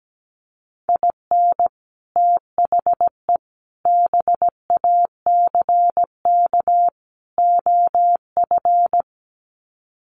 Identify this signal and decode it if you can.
Morse code, heard as a keyed tone, about 17 wpm, 700 Hz. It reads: IN THE BACK OF
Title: Sets of 4 Words 17wpm